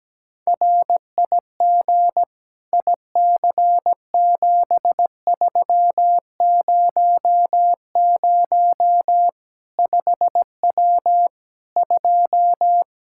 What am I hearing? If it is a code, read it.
Morse code, heard as a keyed tone, about 17 wpm, 700 Hz. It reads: RIG IC7300 5W 2
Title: K1USN SST QSO Practice 17wpm